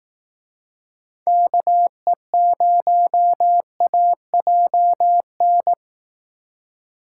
Morse code, heard as a keyed tone, about 18 wpm, 700 Hz. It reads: KE0AJN
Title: Speed Racing - USA 2x3 Call Signs 15wpm